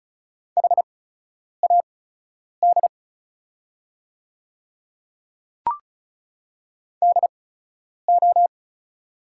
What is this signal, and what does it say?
Morse code, heard as a keyed tone, about 36 wpm, spaced as for 10 wpm, 690 Hz. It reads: HAD DO
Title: 8x Character Spacing - Top 100 Words - 35wpm